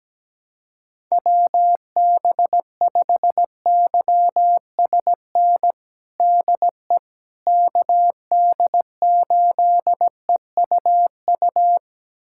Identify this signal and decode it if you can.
Morse code, heard as a keyed tone, about 17 wpm, 700 Hz. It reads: WB5YSN DE KD8EUU
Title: USA Callsign DE Callsign - 6 Characters - 17wpm